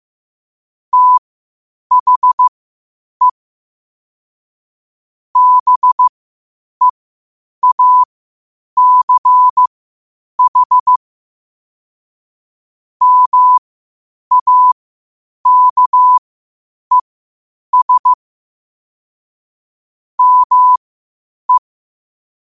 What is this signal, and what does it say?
Morse code, heard as a keyed tone, about 15 wpm, spaced as for 8 wpm, 1000 Hz. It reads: THE BEACH MAKES ME
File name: qotdfiles / 8wpm